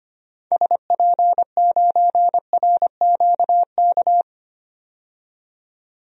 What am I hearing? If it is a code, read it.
Morse code, heard as a keyed tone, about 25 wpm, 700 Hz. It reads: SP9RQK
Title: DXCC-100 Call Signs - 6 Characters - 25wpm